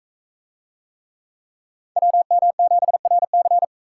Morse code, heard as a keyed tone, about 40 wpm, 700 Hz. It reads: WM7RC